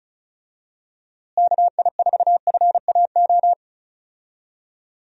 Morse code, heard as a keyed tone, about 34 wpm, 700 Hz. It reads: KI4FAO
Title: USA Call Signs - 6 Characters - Repeated 3x - 35wpm